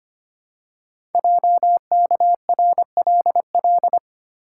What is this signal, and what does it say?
Morse code, heard as a keyed tone, about 25 wpm, 700 Hz. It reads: JKRLL
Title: Code Groups - 5 Letters - 25wpm